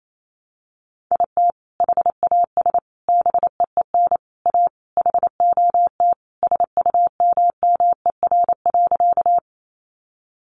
Morse code, heard as a keyed tone, about 28 wpm, 700 Hz. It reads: IT HAS BEEN A HOT SUMMER.
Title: Sentences from Top 1100 Words 28wpm